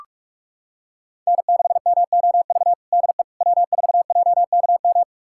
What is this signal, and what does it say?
Morse code, heard as a keyed tone, about 46 wpm, 700 Hz. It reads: N6MOV DE W4JKM
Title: USA Callsign DE Callsign - 5 Characters - 45wpm